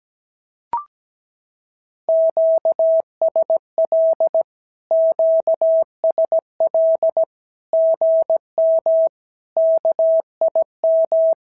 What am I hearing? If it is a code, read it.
Morse code, heard as a keyed tone, about 17 wpm, 650 Hz. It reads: QSL QSL GM KIM
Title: SOTA QSO Practice 17wpm